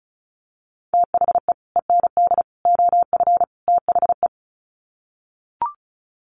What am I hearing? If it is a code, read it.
Morse code, heard as a keyed tone, about 34 wpm, 700 Hz. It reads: THE END OF THE
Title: Sets of 4 Words 35wpm